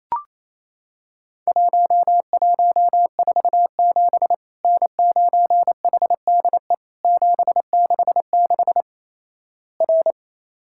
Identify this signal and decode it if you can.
Morse code, heard as a keyed tone, about 28 wpm, 700 Hz. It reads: 1147 N9HDE 766 R
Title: Worked All Europe DX Contest - QTC Exchange - 28wpm